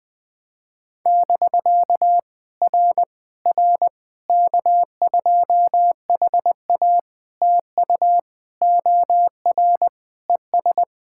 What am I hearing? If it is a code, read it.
Morse code, heard as a keyed tone, about 20 wpm, 700 Hz. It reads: <BK> R R K2HA TU OR ES
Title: POTA QSO Practice - 20wpm